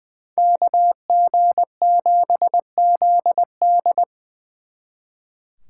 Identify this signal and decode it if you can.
Morse code, heard as a keyed tone, about 24 wpm, 700 Hz. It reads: KG7ZD
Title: Speed Racing - USA Call Signs 20wpm